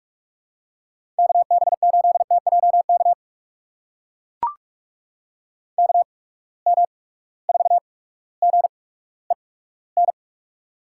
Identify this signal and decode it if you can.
Morse code, heard as a keyed tone, about 42 wpm, spaced as for 19 wpm, 700 Hz. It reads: KB8TJK KM4GEN
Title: USA Call Signs - 6 Characters - 8x Character Spacing - 45wpm